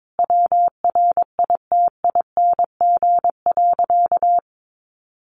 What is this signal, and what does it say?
Morse code, heard as a keyed tone, about 22 wpm, 700 Hz. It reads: WRITING.
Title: Sentences from Top 800 Words 22wpm